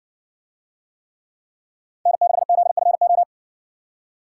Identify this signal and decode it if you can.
Morse code, heard as a keyed tone, about 58 wpm, 700 Hz. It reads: N6ZFO